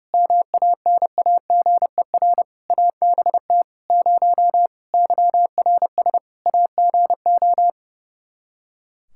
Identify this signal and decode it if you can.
Morse code, heard as a keyed tone, about 30 wpm, 700 Hz. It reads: MANAGER ABT 0 YRS AGO
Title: Practice QSO - Exchange 3 - More Info - 30wpm